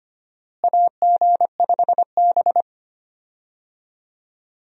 Morse code, heard as a keyed tone, about 25 wpm, 700 Hz. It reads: AG5B
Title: USA Call Signs - 4 Characters - Repeated 2x - 25wpm